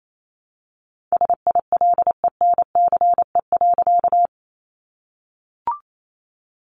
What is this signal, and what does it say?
Morse code, heard as a keyed tone, about 28 wpm, 700 Hz. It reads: SILENCE.
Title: A Sound Of Thunder 28wpm